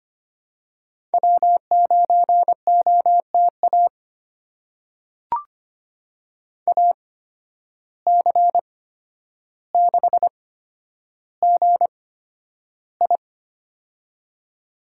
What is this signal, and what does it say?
Morse code, heard as a keyed tone, about 25 wpm, spaced as for 11 wpm, 700 Hz. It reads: W9OTA AC6GI
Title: USA Call Signs - 5 Characters - 8x Character Spacing - 25wpm